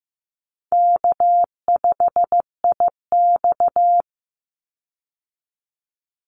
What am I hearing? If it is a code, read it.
Morse code, heard as a keyed tone, about 15 wpm, 700 Hz. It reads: K5IX
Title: Easy Call Signs Repeated 2x - 15wpm